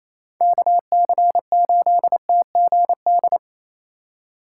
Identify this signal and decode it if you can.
Morse code, heard as a keyed tone, about 36 wpm, 700 Hz. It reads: KC8TGD